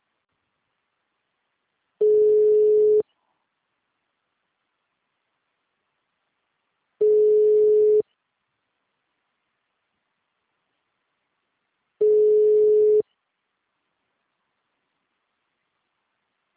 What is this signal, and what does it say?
Ringback tone (Europe, 425 Hz).
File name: out- -2001- - - .2